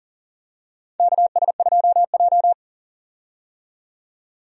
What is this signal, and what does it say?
Morse code, heard as a keyed tone, about 40 wpm, 700 Hz. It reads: KS2J
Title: USA Call Signs - 4 Characters - 40wpm